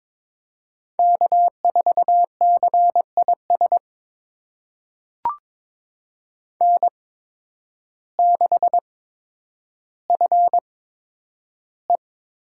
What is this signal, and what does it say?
Morse code, heard as a keyed tone, about 22 wpm, spaced as for 10 wpm, 700 Hz. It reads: K4CIS N6FE
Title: USA Call Signs - 5 Characters - 8x Character Spacing - 22wpm